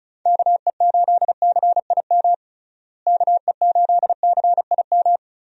Morse code, heard as a keyed tone, about 34 wpm, 700 Hz. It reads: KE8CIM KE8CIM
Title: USA Call Signs - 6 Characters - Repeated 3x - 35wpm